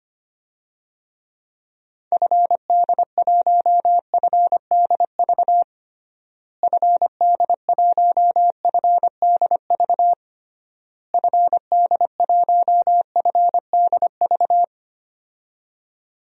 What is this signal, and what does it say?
Morse code, heard as a keyed tone, about 25 wpm, 700 Hz. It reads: FD1FDV FD1FDV FD1FDV
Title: DXCC-100 Call Signs - 6 Characters - Repeated 3x - 25wpm